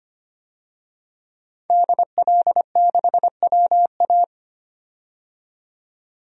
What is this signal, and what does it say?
Morse code, heard as a keyed tone, about 25 wpm, 700 Hz. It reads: DL6WA
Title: DXCC-100 Call Signs - 5 Characters - 25wpm